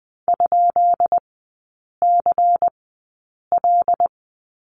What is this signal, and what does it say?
Morse code, heard as a keyed tone, about 20 wpm, spaced as for 8 wpm, 690 Hz. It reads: ?CL
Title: 133 Lesson 22e - Question Mark - Three Characters 20wpm